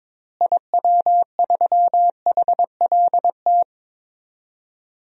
Morse code, heard as a keyed tone, about 22 wpm, 700 Hz. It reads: IW3HLT
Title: DXCC-100 Call Signs - 6 Characters - Repeated 3x - 22wpm